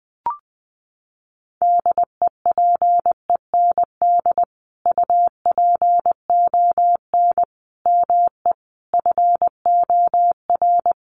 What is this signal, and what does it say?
Morse code, heard as a keyed tone, about 20 wpm, 700 Hz. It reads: DEPEND UPON ME FOR
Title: Sentences from Top 1300 Words 20wpm